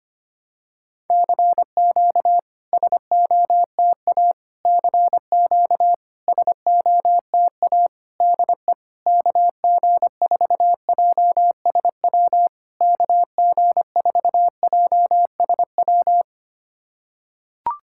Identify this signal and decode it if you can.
Morse code, heard as a keyed tone, about 25 wpm, 660 Hz. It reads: CQ SOTA CQ SOTA DE KG4JSW KG4JSW
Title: SOTA QSO Practice 25wpm